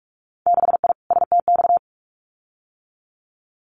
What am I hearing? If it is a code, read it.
Morse code, heard as a keyed tone, about 46 wpm, 700 Hz. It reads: 6I STX
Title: Winter Field Day Exchanges Encore 45wpm